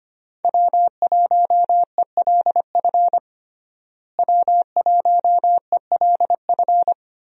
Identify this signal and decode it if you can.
Morse code, heard as a keyed tone, about 25 wpm, 700 Hz. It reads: W1ELF W1ELF
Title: USA Call Signs - 5 Characters - Repeated 2x - 25wpm